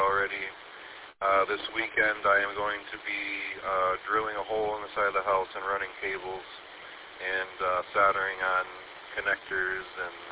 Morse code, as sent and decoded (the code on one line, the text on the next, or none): none